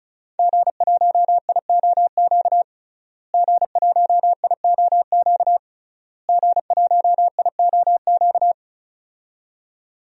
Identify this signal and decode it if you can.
Morse code, heard as a keyed tone, about 36 wpm, 700 Hz. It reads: G1IOQ G1IOQ G1IOQ
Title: DXCC-100 Call Signs - 5 Characters - Repeated 3x - 35wpm